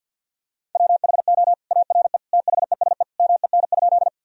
Morse code, heard as a keyed tone, about 52 wpm, 700 Hz. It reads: WHO ARE THESE MEN?
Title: Sentences from Top 1000 Words - Review Part 1 - 50wpm